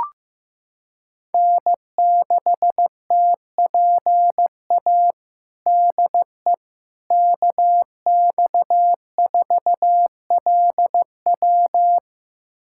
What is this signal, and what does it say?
Morse code, heard as a keyed tone, about 15 wpm, 700 Hz. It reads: N6TPA DE KX4LW
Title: USA Callsign DE Callsign - 5 Characters - 15wpm